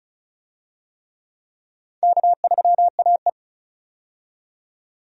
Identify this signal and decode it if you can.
Morse code, heard as a keyed tone, about 34 wpm, 700 Hz. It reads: K3AE
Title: USA Call Signs - 4 Characters - 35wpm